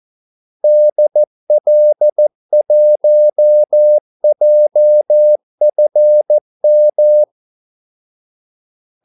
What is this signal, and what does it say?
Morse code, heard as a keyed tone, about 14 wpm, 600 Hz. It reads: DL1JFM
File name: Callsigns-14wpm